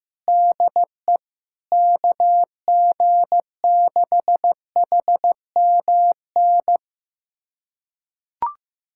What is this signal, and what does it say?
Morse code, heard as a keyed tone, about 15 wpm, 700 Hz. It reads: DE KG6HMN
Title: USA Callsign DE Callsign 15wpm